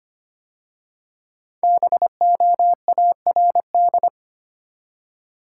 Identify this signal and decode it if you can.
Morse code, heard as a keyed tone, about 25 wpm, 700 Hz. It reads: BOARD
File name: 5 Letter Words - First 3 Letters Given - 25wpm